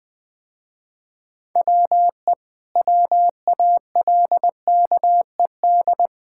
Morse code, heard as a keyed tone, about 20 wpm, 700 Hz. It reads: WE WALKED
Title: Sentences with 4-Word Phrases - Spoken Beginning - 20wpm